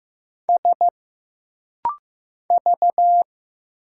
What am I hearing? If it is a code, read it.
Morse code, heard as a keyed tone, about 15 wpm, 700 Hz. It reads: S V
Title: Single Letters - Rapid-Fire 15wpm